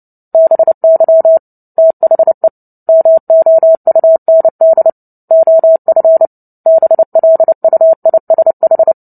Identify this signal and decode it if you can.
Morse code, heard as a keyed tone, about 30 wpm, 650 Hz. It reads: BY THE MOUND OF BLUISH
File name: Book Two / 03-Chapter 3 - 29 WPM